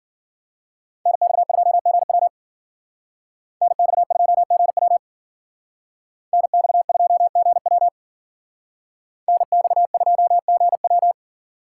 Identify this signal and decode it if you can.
Morse code, heard as a keyed tone, about 49 wpm, 700 Hz. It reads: NX2GW NX2GW NX2GW NX2GW
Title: Speed Racing - USA Call Signs 40wpm